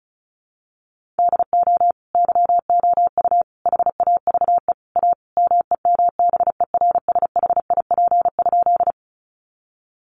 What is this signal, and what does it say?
Morse code, heard as a keyed tone, about 34 wpm, 700 Hz. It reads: DO YOU HAVE A MEMBERSHIP?
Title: Sentences from Top 1400 Words 35wpm